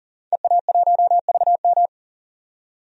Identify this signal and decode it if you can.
Morse code, heard as a keyed tone, about 40 wpm, 700 Hz. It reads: EA1VM
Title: DXCC-100 Call Signs - 5 Characters - 40wpm